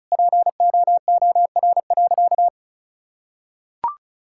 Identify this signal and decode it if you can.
Morse code, heard as a keyed tone, about 34 wpm, 700 Hz. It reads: POOR.
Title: Sentences from Top 1600 Words 35wpm